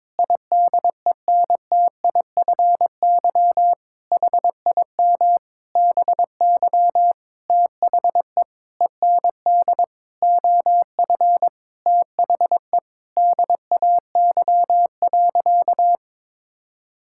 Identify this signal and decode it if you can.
Morse code, heard as a keyed tone, about 22 wpm, 700 Hz. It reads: IDENTIFY HIM BY THE END OF THE DAY.
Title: Sentences from Top 1000 Words - Review Part 6 - 22wpm